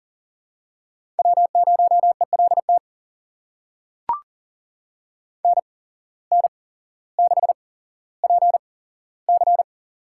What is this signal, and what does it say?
Morse code, heard as a keyed tone, about 40 wpm, spaced as for 17 wpm, 700 Hz. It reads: W0ELT NN6PC
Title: USA Call Signs - 8x Character Spacing - 40wpm